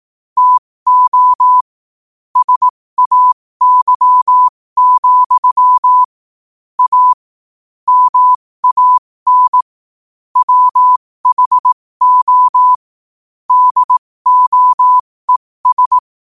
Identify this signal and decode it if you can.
Morse code, heard as a keyed tone, about 18 wpm, spaced as for 15 wpm, 1000 Hz. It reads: TO SAY, A MAN WHO DOES
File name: qotdfiles / 15wpm